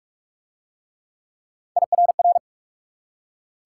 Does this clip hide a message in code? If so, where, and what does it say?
Yes: Morse code from 1.8 to 2.4 s, heard as a keyed tone, about 43 wpm, 700 Hz. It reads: IRR